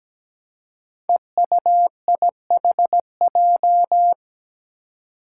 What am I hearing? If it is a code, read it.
Morse code, heard as a keyed tone, about 17 wpm, 700 Hz. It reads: EUIHJ
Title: Five Challenging Letter-Number Combinations 17wpm